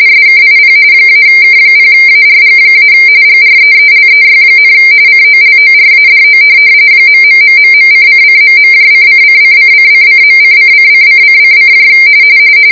RTTY, about 45.45 baud. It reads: SKIP -K0ARG- GREELEY, COLORADO .....:::..::::::::.::::::::